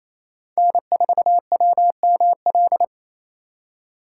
Morse code, heard as a keyed tone, about 35 wpm, 700 Hz. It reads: N4WML